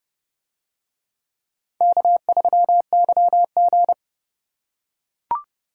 Morse code, heard as a keyed tone, about 28 wpm, 700 Hz. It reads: K3YG